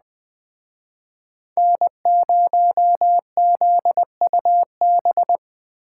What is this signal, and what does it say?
Morse code, heard as a keyed tone, about 21 wpm, 700 Hz. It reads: N0ZUB
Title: Speed Racing - USA 1x3 Call Signs 17wpm